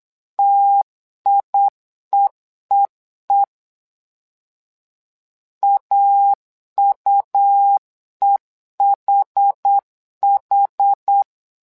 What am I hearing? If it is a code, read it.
Morse code, heard as a keyed tone, about 8 wpm, 800 Hz. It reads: TIEEE AUEHH
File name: LEZIONI / 2 UHDN / 2 UH 8wpm 2129